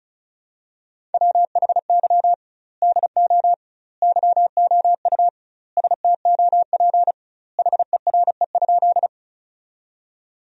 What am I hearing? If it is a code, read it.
Morse code, heard as a keyed tone, about 34 wpm, 700 Hz. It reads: WHY DO YOU STOP HERE?
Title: 2x Word-Spacing - Sentences from Top 300 Words - 35wpm